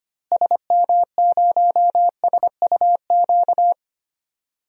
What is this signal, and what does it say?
Morse code, heard as a keyed tone, about 25 wpm, 700 Hz. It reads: SM0SUQ
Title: DXCC-100 Call Signs - 6 Characters - Repeated 3x - 25wpm